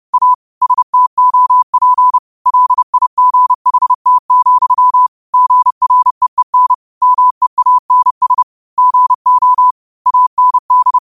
Morse code, heard as a keyed tone, about 30 wpm, 1000 Hz. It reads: A STOP LIGHT, GREEN MEANS GO AND
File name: qotdfiles / 30wpm